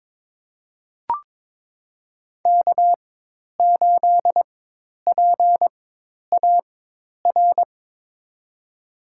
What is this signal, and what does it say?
Morse code, heard as a keyed tone, about 22 wpm, spaced as for 10 wpm, 700 Hz. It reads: K8PAR
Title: USA Call Signs - 5 Characters - 4x Character Spacing - 22wpm